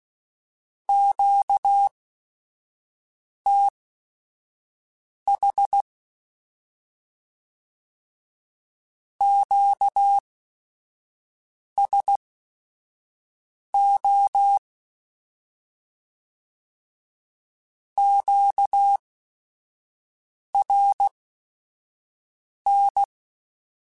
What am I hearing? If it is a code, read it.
Morse code, heard as a keyed tone, about 16 wpm, spaced as for 5 wpm, 800 Hz. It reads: QTH QSO QRN
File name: Q Calls / Q Calls-mp3s / Q Calls Once @5